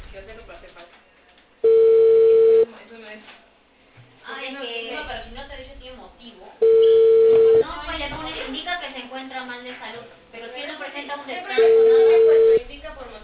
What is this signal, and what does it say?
Ringback tone (China, 450 Hz), with speech.